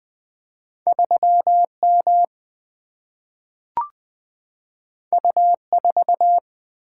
Morse code, heard as a keyed tone, about 20 wpm, 700 Hz. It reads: 3 M U 4
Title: Single-Letter Single-Number 20wpm